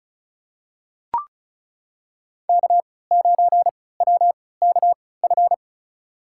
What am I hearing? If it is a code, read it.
Morse code, heard as a keyed tone, about 34 wpm, spaced as for 20 wpm, 690 Hz. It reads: K9WKF
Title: USA Call Signs - 5 Characters - 3x Character Spacing - 35wpm